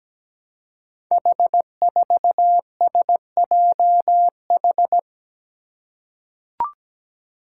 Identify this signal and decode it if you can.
Morse code, heard as a keyed tone, about 17 wpm, 700 Hz. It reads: H4SJH
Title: Five Challenging Letter-Number Combinations 17wpm